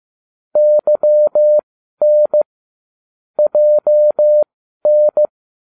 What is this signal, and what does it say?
Morse code, heard as a keyed tone, about 15 wpm, spaced as for 12 wpm, 600 Hz. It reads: YN JN